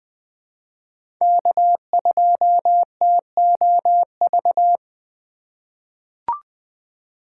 Morse code, heard as a keyed tone, about 20 wpm, 700 Hz. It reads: K2TOV
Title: USA Call Signs - 5 Characters - 4x Character Spacing - 20wpm